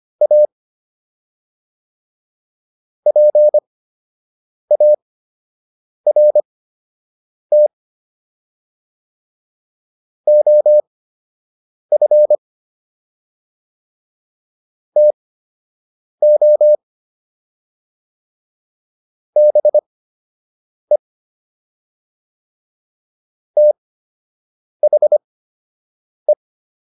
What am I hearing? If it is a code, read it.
Morse code, heard as a keyed tone, about 25 wpm, spaced as for 7 wpm, 600 Hz. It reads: A PART OF TO BE THE